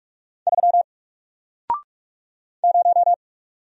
Morse code, heard as a keyed tone, about 44 wpm, 700 Hz. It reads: 3 0
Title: Single Letter-Number Alt 45wpm